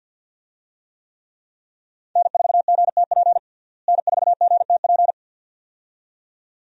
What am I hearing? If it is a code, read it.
Morse code, heard as a keyed tone, about 51 wpm, 700 Hz. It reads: N4GTP N4GTP
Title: USA Call Signs - 5 Characters - Repeated 2x - 50wpm